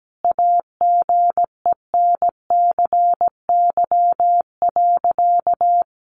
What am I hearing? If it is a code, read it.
Morse code, heard as a keyed tone, about 17 wpm, 700 Hz. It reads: AGENCY.
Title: Sentences from Top 1300 Words 17wpm